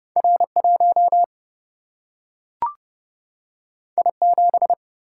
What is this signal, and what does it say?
Morse code, heard as a keyed tone, about 30 wpm, 700 Hz. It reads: R 1 I 7
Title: Single-Letter Single-Number 30wpm